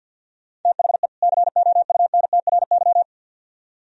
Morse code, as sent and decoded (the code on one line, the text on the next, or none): - .... / -.-. --- ..- -. - .-. -.--
THE COUNTRY